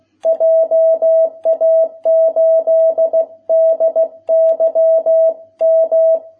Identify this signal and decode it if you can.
Morse code, heard as a keyed tone, about 16 wpm, 650 Hz. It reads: JA8DYM